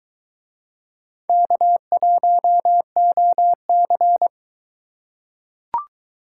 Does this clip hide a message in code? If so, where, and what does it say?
Yes: Morse code from 1.3 to 4.3 s, heard as a keyed tone, about 22 wpm, 700 Hz. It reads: K1OC